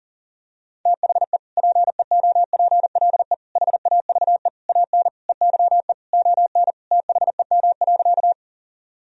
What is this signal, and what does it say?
Morse code, heard as a keyed tone, about 40 wpm, 700 Hz. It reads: THE PEOPLE HAVE AN EYE ON THEM.
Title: Sentences from Top 1000 Words - Review Part 1 - 40wpm